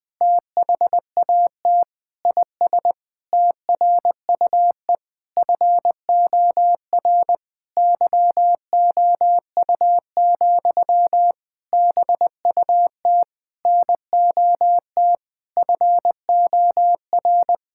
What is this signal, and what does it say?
Morse code, heard as a keyed tone, about 21 wpm, 700 Hz. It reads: THAT IS TRUE FOR YOU, BUT NOT FOR
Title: Sentences from Top 1000 Words - Review Part 5 - 20wpm